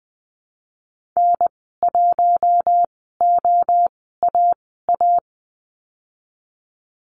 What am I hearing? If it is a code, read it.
Morse code, heard as a keyed tone, about 20 wpm, spaced as for 15 wpm, 700 Hz. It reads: N1OAA
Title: 048 Lesson 9g - Number 4 - Call Signs 20wpm